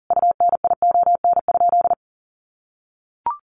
Morse code, heard as a keyed tone, about 40 wpm, 700 Hz. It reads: UNION?
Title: Sentences from Top 1400 Words 40wpm